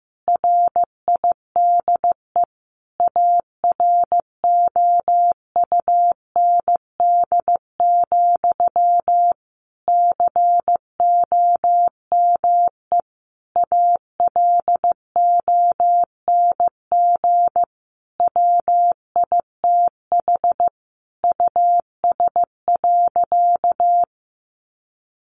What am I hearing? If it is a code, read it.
Morse code, heard as a keyed tone, about 15 wpm, 700 Hz. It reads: RIDE AROUND, COME ALONG WITH US.
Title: Sentences from Top 1200 Words 15wpm